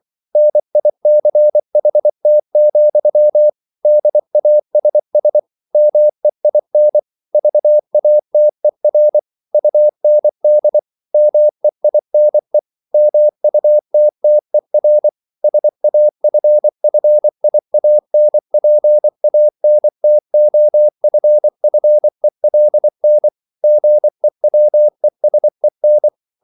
Morse code, heard as a keyed tone, about 24 wpm, 600 Hz. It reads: NICHT, DASS MEIN VATER UND MEINE MUTTER SAFFIANPANTOFFELN GEWESEN